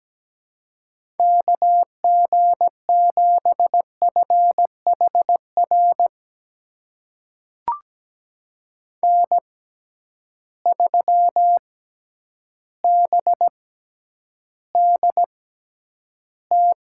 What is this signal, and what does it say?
Morse code, heard as a keyed tone, about 17 wpm, spaced as for 9 wpm, 700 Hz. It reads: KG7FHR N3BDT
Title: USA Call Signs - 6x Character Spacing - 17wpm